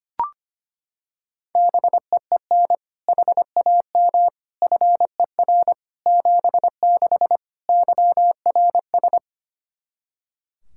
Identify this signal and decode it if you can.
Morse code, heard as a keyed tone, about 25 wpm, 700 Hz. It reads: BEEN HAM FER 76 YRS
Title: K1USN SST QSO Practice 25wpm